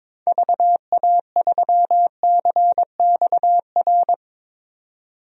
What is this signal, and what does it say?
Morse code, heard as a keyed tone, about 22 wpm, 700 Hz. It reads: VA3CXR
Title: DXCC-100 Call Signs - 6 Characters - 22wpm